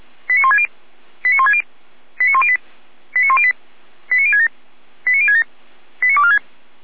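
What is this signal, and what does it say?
Five-tone selective call (ZVEI), heard as quick runs of five tones, tones of about 70 ms. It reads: 89160 89160 89108 89108 89067 89067 89246
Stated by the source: ZVEI five-tone selective call received on an SDR.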